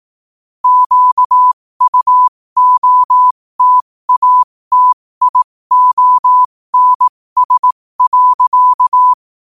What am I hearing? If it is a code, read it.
Morse code, heard as a keyed tone, about 18 wpm, spaced as for 15 wpm, 1000 Hz. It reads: QUOTATIONS.
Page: New quotes every day in morse code at 15 Words per minute.